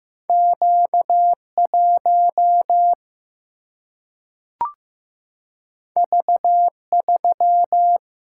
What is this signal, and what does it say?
Morse code, heard as a keyed tone, about 15 wpm, 700 Hz. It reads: Q1 V3
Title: Single-Letter Single-Number 15wpm